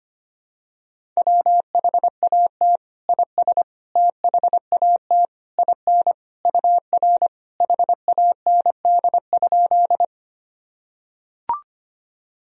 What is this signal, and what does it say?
Morse code, heard as a keyed tone, about 25 wpm, 700 Hz. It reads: WHAT IS THAT IN UR HAND?
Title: Sentences from Top 200 Words and Abbreviations 25wpm